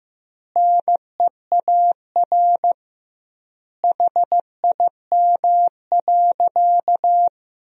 Morse code, heard as a keyed tone, about 15 wpm, 700 Hz. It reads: NEAR HIM.
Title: 2x Word-Spacing - Sentences from Top 200 Words - 15wpm